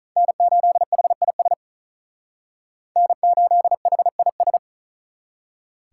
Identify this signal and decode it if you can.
Morse code, heard as a keyed tone, about 45 wpm, 700 Hz. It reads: N8HIS N8HIS